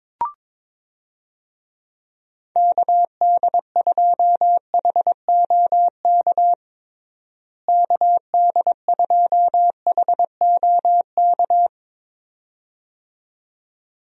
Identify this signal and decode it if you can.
Morse code, heard as a keyed tone, about 22 wpm, 700 Hz. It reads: KD2HOK KD2HOK
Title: USA Call Signs - 6 Characters - Repeated 2x - 22wpm